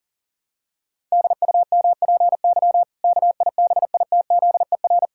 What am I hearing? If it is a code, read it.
Morse code, heard as a keyed tone, about 40 wpm, 700 Hz. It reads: DUMPY KIBITZER
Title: Pangram Challenge v2 - 40wpm